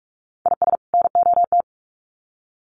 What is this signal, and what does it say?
Morse code, heard as a keyed tone, about 44 wpm, 700 Hz. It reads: IS NOT
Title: Sets of 2 Words 45wpm